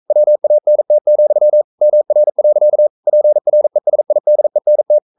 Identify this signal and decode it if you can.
Morse code, heard as a keyed tone, about 42 wpm, 600 Hz. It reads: WANT, MR. PRESIDENT
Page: Romanen Never af Ken Follet på engelsk.